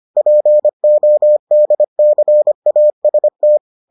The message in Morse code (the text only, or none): PODCAST